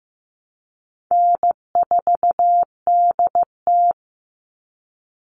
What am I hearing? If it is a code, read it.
Morse code, heard as a keyed tone, about 15 wpm, 700 Hz. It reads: N4DT
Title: Call Signs Repeated 2x - 15wpm